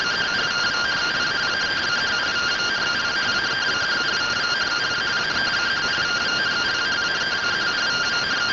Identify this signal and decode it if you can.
Two-tone teleprinter signal, about 170 Hz shift.